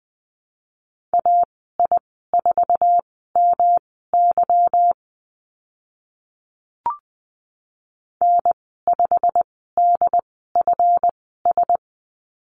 Morse code, heard as a keyed tone, about 20 wpm, spaced as for 14 wpm, 700 Hz. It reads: AI4MY N5DFS
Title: 161 Lesson 26g - Letter g - Call Signs 20wpm